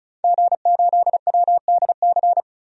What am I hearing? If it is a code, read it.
Morse code, heard as a keyed tone, about 34 wpm, 700 Hz. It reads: G8WDC
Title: DXCC-100 Call Signs - 5 Characters - 35wpm